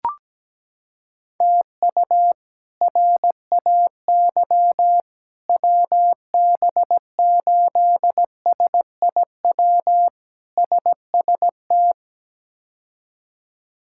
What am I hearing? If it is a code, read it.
Morse code, heard as a keyed tone, about 17 wpm, 700 Hz. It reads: TU RAY WB8SIW SST
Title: K1USN SST QSO Practice - 17wpm